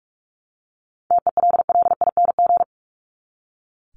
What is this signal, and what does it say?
Morse code, heard as a keyed tone, about 44 wpm, 700 Hz. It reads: TELLING
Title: ING Suffix - Top 100 Words with Base Words 45wpm